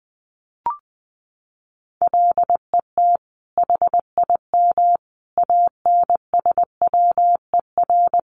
Morse code, heard as a keyed tone, about 20 wpm, 700 Hz. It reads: LET HIM ANSWER
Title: Sentences from Top 300 Words 20wpm